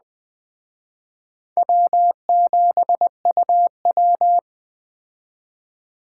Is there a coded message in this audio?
Yes: Morse code from 1.6 to 4.4 s, heard as a keyed tone, about 19 wpm, 700 Hz. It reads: W7UW